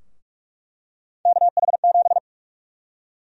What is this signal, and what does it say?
Morse code, heard as a keyed tone, about 46 wpm, 700 Hz. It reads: KH7
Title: USA Call Sign Prefixes - 45wpm